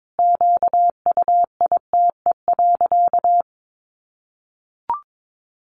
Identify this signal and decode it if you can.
Morse code, heard as a keyed tone, about 22 wpm, 700 Hz. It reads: QUITE.
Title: Sentences from Top 1300 Words 22wpm